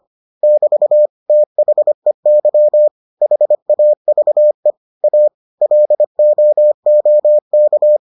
Morse code, heard as a keyed tone, about 25 wpm, 600 Hz. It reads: <BT> THEY HAVE A LOOK